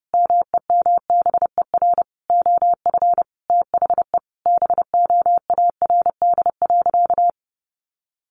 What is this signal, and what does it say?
Morse code, heard as a keyed tone, about 30 wpm, 700 Hz. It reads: MEMBER OF THE BOARD.
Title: Sentences from Top 1100 Words 30wpm